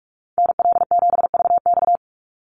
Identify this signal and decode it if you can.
Morse code, heard as a keyed tone, about 44 wpm, 700 Hz. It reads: NL7VX
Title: Call Signs - 45wpm